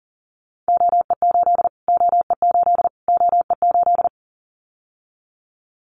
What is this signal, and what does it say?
Morse code, heard as a keyed tone, about 40 wpm, 700 Hz. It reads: OE8 OE8 OE8
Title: Callsign Prefixes for Top 25 DXCC - Repeated 3x - 40wpm